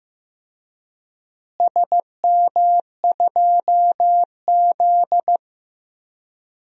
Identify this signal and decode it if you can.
Morse code, heard as a keyed tone, about 15 wpm, 700 Hz. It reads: SM2Z
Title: DXCC-100 Call Signs - 4 Characters - Repeated 3x - 15wpm